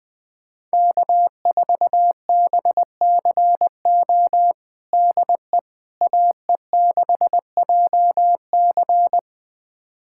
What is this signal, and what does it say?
Morse code, heard as a keyed tone, about 20 wpm, 700 Hz. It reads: K4BCO DE AE6JC
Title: USA Callsign DE Callsign - 5 Characters - 20wpm